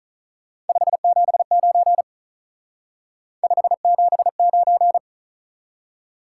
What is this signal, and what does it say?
Morse code, heard as a keyed tone, about 43 wpm, 700 Hz. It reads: 579 579
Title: Speed-Racing Top 180 QSO Elements 35wpm